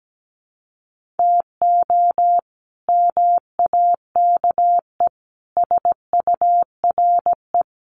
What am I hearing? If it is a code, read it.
Morse code, heard as a keyed tone, about 17 wpm, 700 Hz. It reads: TO MAKE SURE
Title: Sets of 4 Words 17wpm